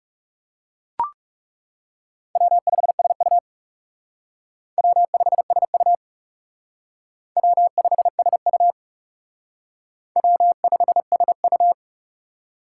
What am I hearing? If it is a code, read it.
Morse code, heard as a keyed tone, about 37 wpm, 700 Hz. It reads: W5SU W5SU W5SU W5SU
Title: Speed Racing - USA 1x2 Call Signs 30wpm